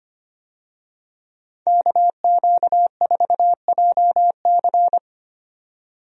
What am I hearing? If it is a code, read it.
Morse code, heard as a keyed tone, about 25 wpm, 690 Hz. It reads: KQ4JC
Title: USA Call Signs - 5 Characters - 25wpm